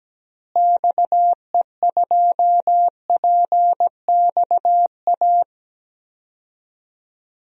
Morse code, heard as a keyed tone, about 17 wpm, 700 Hz. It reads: XE2PXA
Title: DXCC-100 Call Signs - 6 Characters - Repeated 3x - 17wpm